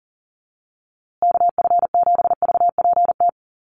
Morse code, heard as a keyed tone, about 40 wpm, 700 Hz. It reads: KF7VPT
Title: USA Call Signs Repeated 3x 40wpm